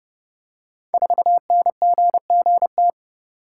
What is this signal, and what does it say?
Morse code, heard as a keyed tone, about 29 wpm, 700 Hz. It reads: 4NGGT
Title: Code Groups - 5 Letter:Numbers - 30wpm